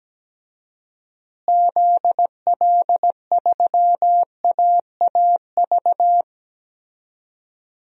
Morse code, heard as a keyed tone, about 17 wpm, 700 Hz. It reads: ZL3AAV
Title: DXCC-100 Call Signs - 6 Characters - Repeated 2x - 17wpm